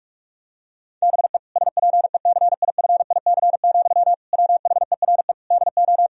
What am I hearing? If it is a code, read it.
Morse code, heard as a keyed tone, about 43 wpm, 700 Hz. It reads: BE SPECIFIC, WHERE DO
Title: Sentences from Top 1000 Words - Review Part 3 - 45wpm